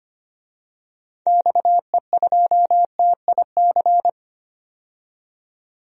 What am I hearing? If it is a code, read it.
Morse code, heard as a keyed tone, about 25 wpm, 700 Hz. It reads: XE2TIC
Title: DXCC-100 Call Signs - 6 Characters - 25wpm